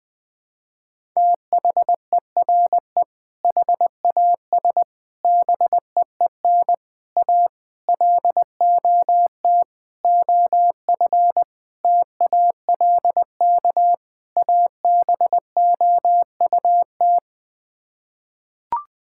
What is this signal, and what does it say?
Morse code, heard as a keyed tone, about 20 wpm, 700 Hz. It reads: THERE HAS BEEN A LOT OF TALK ABOUT
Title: Sets of 8 Words - 20wpm